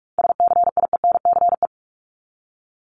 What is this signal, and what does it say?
Morse code, heard as a keyed tone, about 44 wpm, 700 Hz. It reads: SCIENCE
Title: Top 500 Words 45wpm